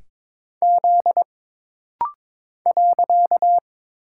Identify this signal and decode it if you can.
Morse code, heard as a keyed tone, about 22 wpm, 700 Hz. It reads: Z .
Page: This rapid-fire timing is geared to push your ICR (Instant Character Recognition) skills.